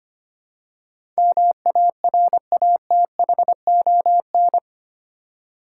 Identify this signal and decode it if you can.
Morse code, heard as a keyed tone, about 25 wpm, 700 Hz. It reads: MARATHON
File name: Eight Letter Words - 25wpm